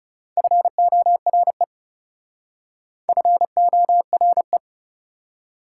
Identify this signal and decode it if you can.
Morse code, heard as a keyed tone, about 38 wpm, 690 Hz. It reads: FORE FORE